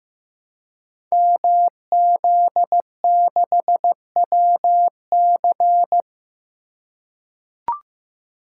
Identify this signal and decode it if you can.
Morse code, heard as a keyed tone, about 15 wpm, 700 Hz. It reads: MZ6WC
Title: Code Groups - 5 Letter:Numbers - 15wpm